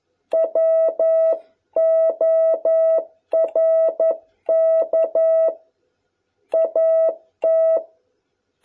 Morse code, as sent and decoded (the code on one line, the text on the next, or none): .-- --- .-. -.- / .- -
WORK AT